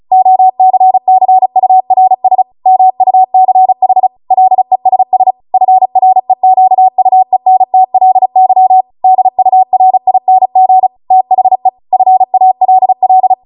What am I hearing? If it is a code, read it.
Morse code, heard as a keyed tone, about 36 wpm, 750 Hz. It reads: OCCURS MUCH LESS FREQUENTLY DURING THE FALL